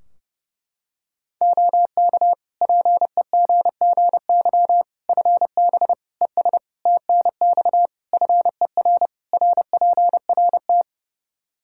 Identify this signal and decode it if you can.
Morse code, heard as a keyed tone, about 30 wpm, 700 Hz. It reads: OK PEGGY FB ES TNX FER RPRT
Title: Practice QSO - Exchange 2 - Info - 30wpm